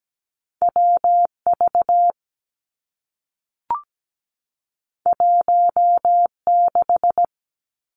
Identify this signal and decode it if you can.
Morse code, heard as a keyed tone, about 17 wpm, 700 Hz. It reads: WV 16
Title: Two Challenging Letter-Number Combinations 17wpm